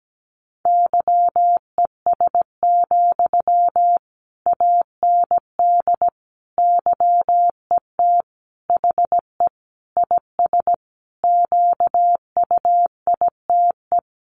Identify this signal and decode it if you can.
Morse code, heard as a keyed tone, about 17 wpm, 700 Hz. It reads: YES, AND YET HE IS QUITE
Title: Sentences from Top 700 Words 17wpm